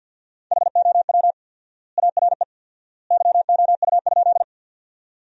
Morse code, heard as a keyed tone, about 53 wpm, 700 Hz. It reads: HOW ARE YOU?
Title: 4x Word-Spacing - Sentences from Top 100 Words - 50wpm